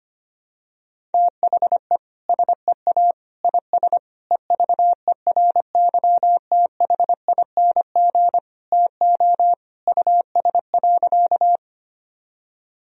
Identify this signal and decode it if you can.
Morse code, heard as a keyed tone, about 25 wpm, 700 Hz. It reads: THE SEA IS EVERYTHING TO US.
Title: Sentences from Top 1000 Words - Review Part 1 - 25wpm